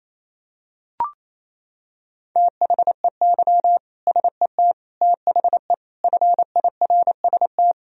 Morse code, heard as a keyed tone, about 28 wpm, 700 Hz. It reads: THEY SET THE FIRST
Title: Sentences from Top 1000 Words - Review Part 1 - 28wpm